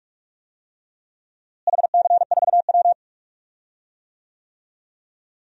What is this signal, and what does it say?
Morse code, heard as a keyed tone, about 46 wpm, 700 Hz. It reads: HC4W
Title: DXCC-100 Call Signs - 4 Characters - Repeated 3x - 45wpm